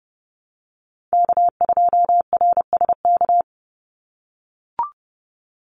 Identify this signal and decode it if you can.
Morse code, heard as a keyed tone, about 30 wpm, 700 Hz. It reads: K2RSK
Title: Call Signs - 30wpm